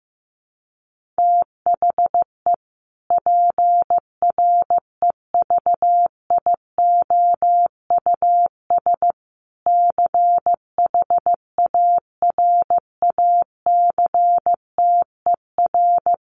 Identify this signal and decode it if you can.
Morse code, heard as a keyed tone, about 15 wpm, 700 Hz. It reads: THE PREVIOUS CHARACTER
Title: Sentences from Top 1500 Words 15wpm